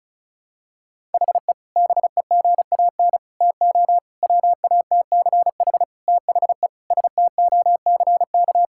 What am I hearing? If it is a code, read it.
Morse code, heard as a keyed tone, about 36 wpm, 700 Hz. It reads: HE BEGAN TO WATCH THE STOCK
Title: Sentences from Top 1000 Words - Review Part 6 - 35wpm